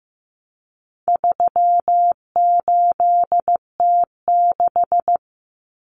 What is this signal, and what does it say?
Morse code, heard as a keyed tone, about 15 wpm, 700 Hz. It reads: 38T6
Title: Common Cut Numbers - 4-Digit 15wpm